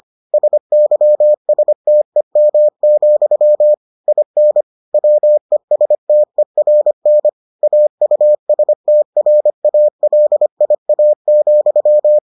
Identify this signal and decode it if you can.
Morse code, heard as a keyed tone, about 25 wpm, 600 Hz. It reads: SYSTEM, IN WESTERN AUSTRALIA,